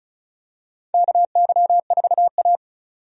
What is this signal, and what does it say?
Morse code, heard as a keyed tone, about 36 wpm, 700 Hz. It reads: KY4A
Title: USA Call Signs - 4 Characters - 35wpm